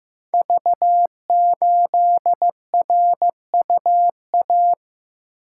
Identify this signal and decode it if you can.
Morse code, heard as a keyed tone, about 15 wpm, 700 Hz. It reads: V8RUA
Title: Code Groups - 5 Letter:Numbers - 15wpm